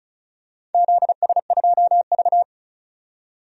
Morse code, heard as a keyed tone, about 34 wpm, 700 Hz. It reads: ZS2V
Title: DXCC-100 Call Signs - 4 Characters - 35wpm